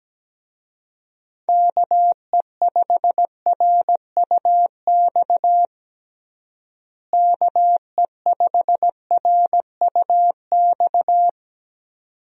Morse code, heard as a keyed tone, about 17 wpm, 700 Hz. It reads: KE5RUX KE5RUX
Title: USA Call Signs - 6 Characters - Repeated 2x - 17wpm